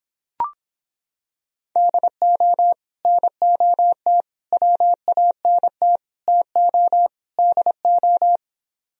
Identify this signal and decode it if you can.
Morse code, heard as a keyed tone, about 26 wpm, 700 Hz. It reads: DO NOT WANT TO DO